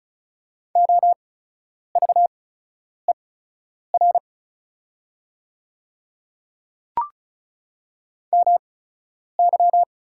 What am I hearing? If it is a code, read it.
Morse code, heard as a keyed tone, about 36 wpm, spaced as for 10 wpm, 700 Hz. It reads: OVER MY
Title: 8x Character Spacing - Top 100 Words - 35wpm